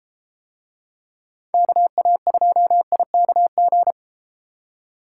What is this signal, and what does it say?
Morse code, heard as a keyed tone, about 32 wpm, 700 Hz. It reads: KA2IKG